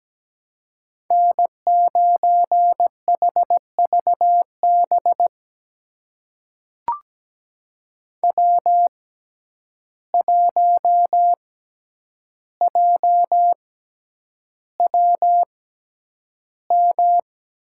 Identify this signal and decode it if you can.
Morse code, heard as a keyed tone, about 17 wpm, spaced as for 9 wpm, 700 Hz. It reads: N9HVB W1JWM
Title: USA Call Signs - 6x Character Spacing - 17wpm